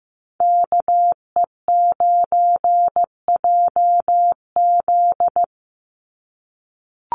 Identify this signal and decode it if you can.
Morse code, heard as a keyed tone, about 15 wpm, 700 Hz. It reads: KE9JZ
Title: USA Call Signs 15wpm